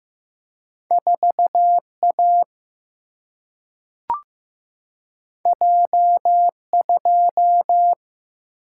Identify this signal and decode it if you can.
Morse code, heard as a keyed tone, about 15 wpm, 700 Hz. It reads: 4 A J 2
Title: Single-Letter Single-Number 15wpm